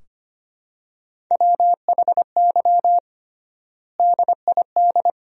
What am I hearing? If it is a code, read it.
Morse code, heard as a keyed tone, about 25 wpm, 700 Hz. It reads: WHY DID
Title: 3x Word-Spacing - Sentences from Top 500 Words - 25wpm